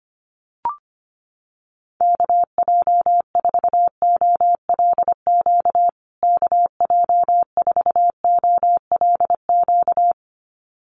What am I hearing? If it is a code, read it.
Morse code, heard as a keyed tone, about 25 wpm, 700 Hz. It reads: KJ4OLQ KJ4OLQ
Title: USA Call Signs Repeated 2x 25wpm